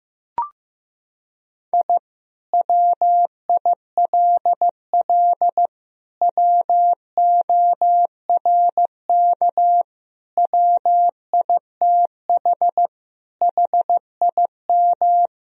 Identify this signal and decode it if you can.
Morse code, heard as a keyed tone, about 15 wpm, 700 Hz. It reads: I WILL WORK WITH HIM
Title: Sentences from Top 1000 Words - Review Part 4 - 15wpm